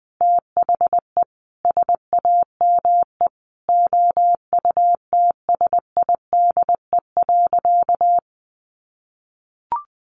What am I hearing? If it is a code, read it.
Morse code, heard as a keyed tone, about 20 wpm, 700 Hz. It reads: THE SAME OUTSIDE.
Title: Sentences from Top 900 Words 20wpm